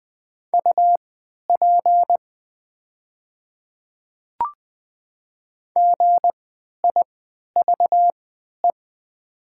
Morse code, heard as a keyed tone, about 20 wpm, spaced as for 11 wpm, 700 Hz. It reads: UP GIVE
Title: 3x Character Spacing - Top 200 Words - 20wpm